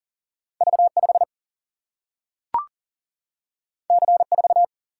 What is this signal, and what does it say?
Morse code, heard as a keyed tone, about 39 wpm, 700 Hz. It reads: V 5 C 4
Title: Single-Letter Single-Number 40wpm